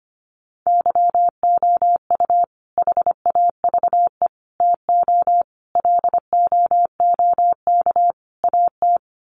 Morse code, heard as a keyed tone, about 25 wpm, 700 Hz. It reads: YOU HAVE TO LOOK AT
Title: Sets of 5 Words Encore 25wpm